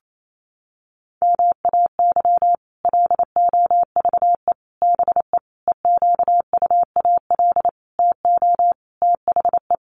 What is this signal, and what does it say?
Morse code, heard as a keyed tone, about 28 wpm, 700 Hz. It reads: MAY LOVE BE EQUAL TO THE
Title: Sentences from Top 1400 Words 28wpm